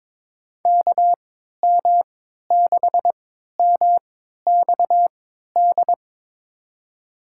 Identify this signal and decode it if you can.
Morse code, heard as a keyed tone, about 22 wpm, spaced as for 12 wpm, 700 Hz. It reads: KM6MXD
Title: USA Call Signs - 6 Characters - 3x Character Spacing - 22wpm